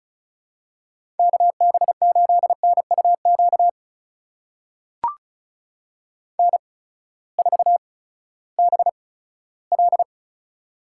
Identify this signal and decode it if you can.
Morse code, heard as a keyed tone, about 36 wpm, spaced as for 18 wpm, 700 Hz. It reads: KB8NUQ N4BL
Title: USA Call Signs - 8x Character Spacing - 35wpm